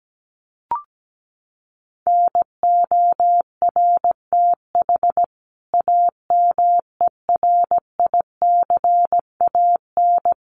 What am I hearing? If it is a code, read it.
Morse code, heard as a keyed tone, about 17 wpm, 700 Hz. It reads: NORTH AMERICAN
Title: Sets of 5 Words Encore 17wpm